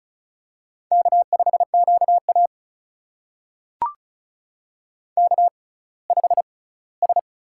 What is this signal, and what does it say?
Morse code, heard as a keyed tone, about 34 wpm, spaced as for 20 wpm, 700 Hz. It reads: K5QA K5S
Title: USA Call Signs - 6x Character Spacing - 35wpm